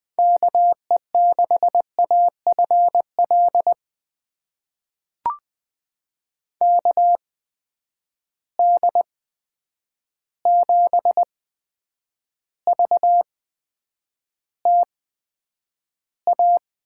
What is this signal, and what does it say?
Morse code, heard as a keyed tone, about 20 wpm, spaced as for 9 wpm, 700 Hz. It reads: KE6AFL KD7VTA
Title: USA Call Signs - 6 Characters - 8x Character Spacing - 20wpm